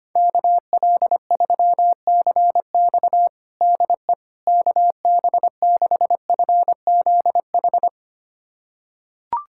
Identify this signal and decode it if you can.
Morse code, heard as a keyed tone, about 25 wpm, 700 Hz. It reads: KL3CX DE KB6FZH
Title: USA Callsign DE Callsign 25wpm